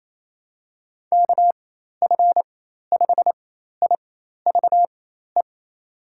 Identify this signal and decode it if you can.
Morse code, heard as a keyed tone, about 27 wpm, spaced as for 13 wpm, 700 Hz. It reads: KF5IVE
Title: USA Call Signs - 6 Characters - 4x Character Spacing - 28wpm